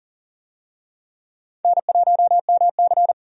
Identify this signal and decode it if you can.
Morse code, heard as a keyed tone, about 40 wpm, 700 Hz. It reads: N1MC
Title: USA Call Signs - 4 Characters - 40wpm